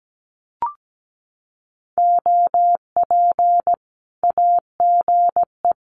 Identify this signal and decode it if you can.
Morse code, heard as a keyed tone, about 17 wpm, 700 Hz. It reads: OP AGE
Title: Common QSO Phrases 17wpm